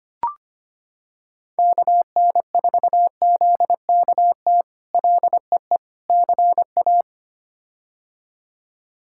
Morse code, heard as a keyed tone, about 25 wpm, 700 Hz. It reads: KN4ZKT LEE CA
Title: K1USN SST QSO Practice - 25wpm